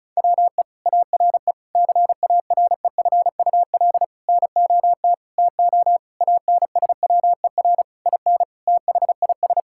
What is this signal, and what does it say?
Morse code, heard as a keyed tone, about 36 wpm, 700 Hz. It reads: WE ARE CAREFUL NOT TO ANSWER IN THIS
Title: Sentences from Top 1600 Words 35wpm